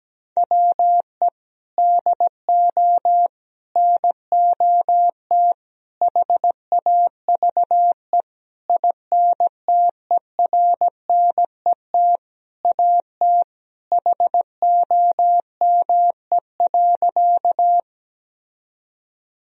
Morse code, heard as a keyed tone, about 17 wpm, 700 Hz. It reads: WE DO NOT HAVE INTERNET AT HOME.
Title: Sentences from Top 1000 Words - Review Part 4 - 17wpm